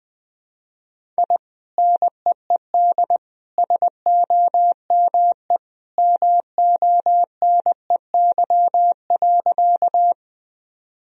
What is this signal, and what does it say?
Morse code, heard as a keyed tone, about 20 wpm, 700 Hz. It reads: I NEED SOME MONEY.
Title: Sentences from Top 1000 Words - Review Part 1 - 20wpm